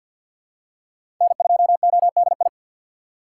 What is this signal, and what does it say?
Morse code, heard as a keyed tone, about 46 wpm, 700 Hz. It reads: N2ODI
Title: USA Call Signs - 6x Character Spacing - 50wpm